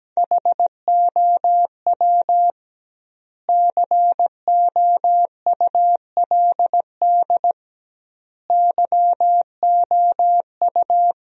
Morse code, heard as a keyed tone, about 17 wpm, 700 Hz. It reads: HOW COULD YOU
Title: 2x Word-Spacing - Sentences from Top 300 Words - 17wpm